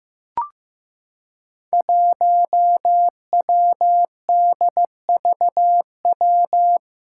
Morse code, heard as a keyed tone, about 15 wpm, 700 Hz. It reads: 1WDVW
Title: Five Challenging Letter-Number Combinations 15wpm